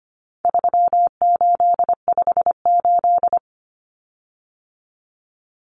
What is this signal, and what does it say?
Morse code, heard as a keyed tone, about 25 wpm, 700 Hz. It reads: 3858
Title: Common Cut Numbers - 4-Digit 25wpm